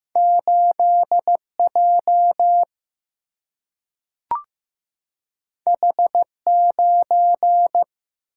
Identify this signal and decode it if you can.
Morse code, heard as a keyed tone, about 15 wpm, 700 Hz. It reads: 8 J H 9
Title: Single-Letter Single-Number 15wpm